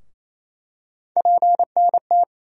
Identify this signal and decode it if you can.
Morse code, heard as a keyed tone, about 28 wpm, 700 Hz. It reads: PNT
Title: USA Call Sign Postfixes - 3 Letters 28wpm